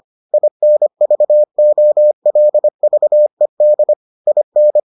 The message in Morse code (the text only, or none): INVOLVED IN